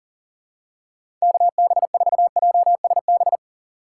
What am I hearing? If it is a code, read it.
Morse code, heard as a keyed tone, about 40 wpm, 700 Hz. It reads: KB4JSB
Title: USA Call Signs - 6 Characters - 3x Character Spacing - 40wpm